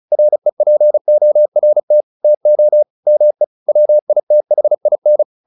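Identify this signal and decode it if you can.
Morse code, heard as a keyed tone, about 35 wpm, 600 Hz. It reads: REPORT TO ME WITHIN
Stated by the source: Morse practice at 35 wpm from CW med Gnister, sending Ken Follett's novel Never in English.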